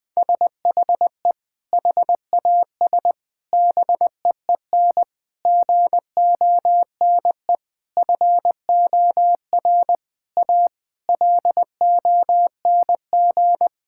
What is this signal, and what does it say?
Morse code, heard as a keyed tone, about 20 wpm, 700 Hz. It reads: SHE HAS BEEN GONE FOR A LONG
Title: Sentences with 3-Word Phrases - Spoken Beginning - 20wpm